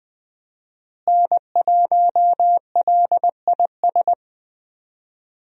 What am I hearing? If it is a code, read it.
Morse code, heard as a keyed tone, about 20 wpm, 700 Hz. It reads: N1LIS
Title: USA Call Signs - 5 Characters - Repeated 3x - 20wpm